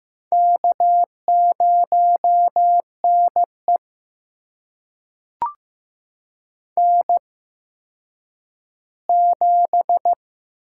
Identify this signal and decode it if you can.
Morse code, heard as a keyed tone, about 15 wpm, spaced as for 9 wpm, 700 Hz. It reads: K0NE N7
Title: USA Call Signs - 4 Characters - 8x Character Spacing - 15wpm